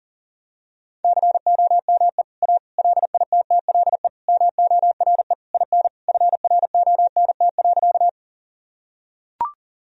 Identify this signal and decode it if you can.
Morse code, heard as a keyed tone, about 37 wpm, 700 Hz. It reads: COME A LITTLE MORE IN FRONT.
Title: Sentences from Top 1000 Words - Review Part 2 - 40wpm